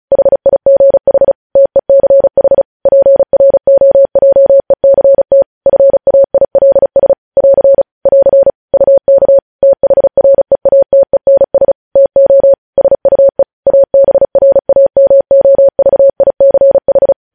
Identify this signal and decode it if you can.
Morse code, heard as a keyed tone, about 36 wpm, 550 Hz. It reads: HIGH TECH PROJECT FAILS <AR> <AR> UK THREATENS TO SUE ABRAMOVICH
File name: podcast / News headlines-1x-35x35